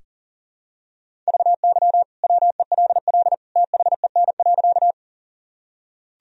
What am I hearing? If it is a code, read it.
Morse code, heard as a keyed tone, about 40 wpm, 700 Hz. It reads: VY WELL THEN.
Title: Sentences from Top 200 Words and Abbreviations 40wpm